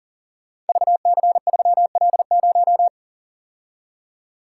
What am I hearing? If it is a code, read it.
Morse code, heard as a keyed tone, about 38 wpm, 700 Hz. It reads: VC3L0
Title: Code Groups - 5 Letter:Numbers - 40wpm